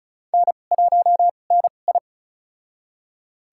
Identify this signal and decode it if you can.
Morse code, heard as a keyed tone, about 36 wpm, spaced as for 25 wpm, 700 Hz. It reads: N1NI
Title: USA Call Signs - 4 Characters - 2x Character Spacing - 35wpm